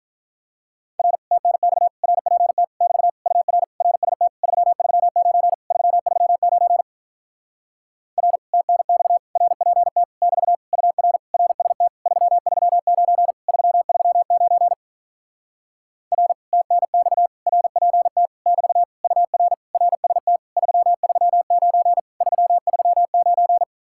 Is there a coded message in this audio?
Yes: Morse code from 1.0 to 23.6 s, heard as a keyed tone, about 48 wpm, 700 Hz. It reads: R TNX RPT <BT> UR RST 339 339 R TNX RPT <BT> UR RST 339 339 R TNX RPT <BT> UR RST 339 339